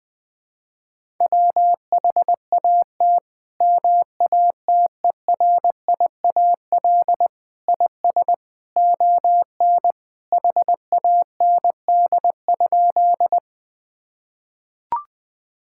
Morse code, heard as a keyed tone, about 20 wpm, 700 Hz. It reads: WHAT MATERIAL IS ON HAND?
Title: Sentences from Top 1000 Words - Review Part 5 - 20wpm